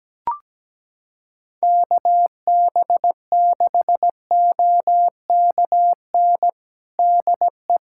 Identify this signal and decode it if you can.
Morse code, heard as a keyed tone, about 17 wpm, 700 Hz. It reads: KB6OKN DE
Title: USA Callsign DE Callsign - 6 Characters - 17wpm